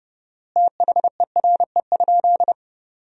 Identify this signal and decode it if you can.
Morse code, heard as a keyed tone, about 32 wpm, 700 Hz. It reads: THERE?
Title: 4x Word-Spacing - Sentences from Top 100 Words - 30wpm